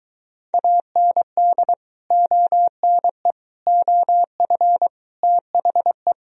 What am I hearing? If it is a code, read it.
Morse code, heard as a keyed tone, about 23 wpm, 700 Hz. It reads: AND ONE OF THE